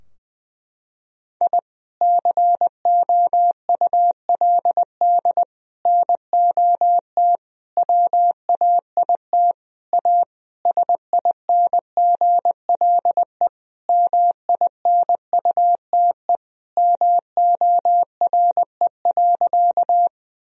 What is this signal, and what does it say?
Morse code, heard as a keyed tone, about 21 wpm, 700 Hz. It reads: I COULD NOT WAIT A SINGLE MINUTE MORE.
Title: Sentences from Top 1000 Words - Review Part 4 - 20wpm